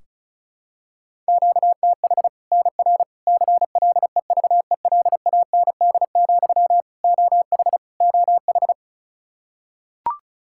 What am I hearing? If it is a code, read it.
Morse code, heard as a keyed tone, about 36 wpm, 700 Hz. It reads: QTH NR CLEVELAND, OH OH
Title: K1USN SST QSO Practice 35wpm